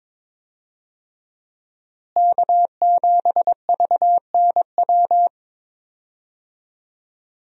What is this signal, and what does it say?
Morse code, heard as a keyed tone, about 22 wpm, 700 Hz. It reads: K7VNW
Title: USA Call Signs - 5 Characters - Repeated 2x - 22wpm